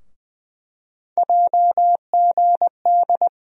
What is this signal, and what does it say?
Morse code, heard as a keyed tone, about 21 wpm, 700 Hz. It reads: JGD
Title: USA Call Sign Postfixes - 3 Letters 20wpm